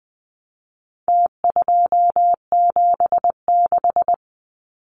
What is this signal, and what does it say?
Morse code, heard as a keyed tone, about 20 wpm, 700 Hz. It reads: T276
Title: Numbers - Contest Style - 4-Digit 20wpm